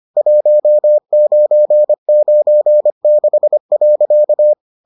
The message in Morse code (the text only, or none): 1996.